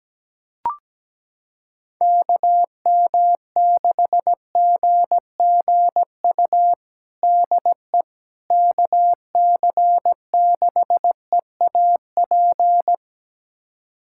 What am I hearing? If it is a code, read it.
Morse code, heard as a keyed tone, about 17 wpm, 700 Hz. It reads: KM6GGU DE KC6EAP
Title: USA Callsign DE Callsign 17wpm